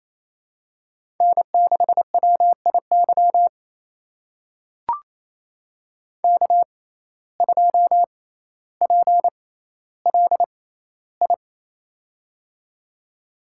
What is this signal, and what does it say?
Morse code, heard as a keyed tone, about 28 wpm, spaced as for 14 wpm, 700 Hz. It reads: N6WIY K2PLI
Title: USA Call Signs - 5 Characters - 6x Character Spacing - 28wpm